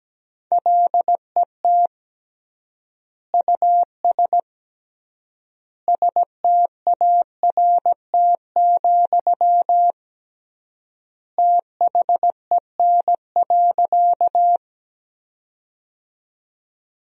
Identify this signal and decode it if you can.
Morse code, heard as a keyed tone, about 17 wpm, 700 Hz. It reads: LET US START, THEN.
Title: 3x Word-Spacing - Sentences from Top 300 Words - 17wpm